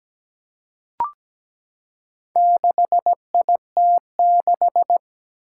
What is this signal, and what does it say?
Morse code, heard as a keyed tone, about 17 wpm, 700 Hz. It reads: 6IT6
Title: Four Challenging Letter-Number Combinations 17wpm